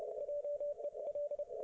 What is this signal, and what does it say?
Morse code, heard as a keyed tone, about 31 wpm, 580 Hz. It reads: JEL